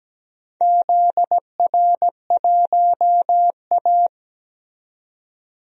Morse code, heard as a keyed tone, about 17 wpm, 700 Hz. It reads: ZR1A
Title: DXCC-100 Call Signs - 4 Characters - 17wpm